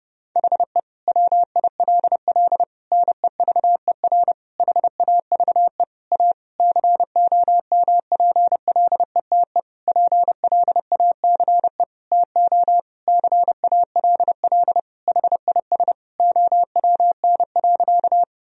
Morse code, heard as a keyed tone, about 32 wpm, 700 Hz. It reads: HE WILL NEVER HAVE A COMPLETE PLACE TO CALL HIS OWN.
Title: Sentences from Top 1000 Words - Review Part 2 - 30wpm